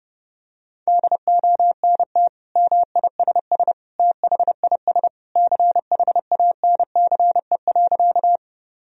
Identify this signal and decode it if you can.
Morse code, heard as a keyed tone, about 32 wpm, 700 Hz. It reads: DONT MISS THIS CHANCE.
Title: Sentences from Top 1000 Words - Review Part 3 - 30wpm